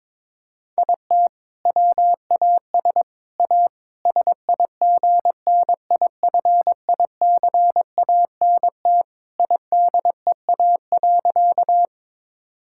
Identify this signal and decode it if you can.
Morse code, heard as a keyed tone, about 22 wpm, 700 Hz. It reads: IT WAS A SIGNIFICANT IDEA.
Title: Sentences from Top 1000 Words - Review Part 6 - 22wpm